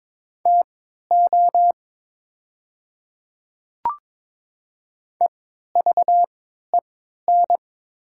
Morse code, heard as a keyed tone, about 22 wpm, spaced as for 13 wpm, 690 Hz. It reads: TO EVEN
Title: 3x Character Spacing - Top 200 Words - 22wpm